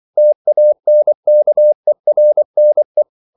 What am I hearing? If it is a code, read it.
Morse code, heard as a keyed tone, about 24 wpm, 600 Hz. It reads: TANKERNE
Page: Ferie 24 wpm